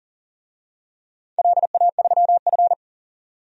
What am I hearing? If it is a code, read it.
Morse code, heard as a keyed tone, about 40 wpm, 700 Hz. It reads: LA3F
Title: DXCC-100 Call Signs - 4 Characters - 40wpm